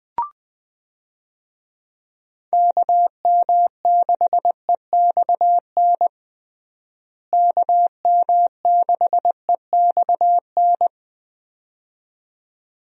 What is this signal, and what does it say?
Morse code, heard as a keyed tone, about 20 wpm, 700 Hz. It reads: KM6EXN KM6EXN
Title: USA Call Signs - 6 Characters - Repeated 2x - 20wpm